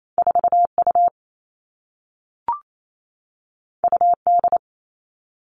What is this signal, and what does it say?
Morse code, heard as a keyed tone, about 28 wpm, 700 Hz. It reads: 4 U U D
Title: Two Challenging Letter-Number Combinations 28wpm